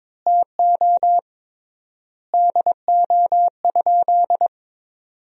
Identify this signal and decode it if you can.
Morse code, heard as a keyed tone, about 23 wpm, 700 Hz. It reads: TO DO?
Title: 3x Word-Spacing - Sentences from Top 100 Words - 22wpm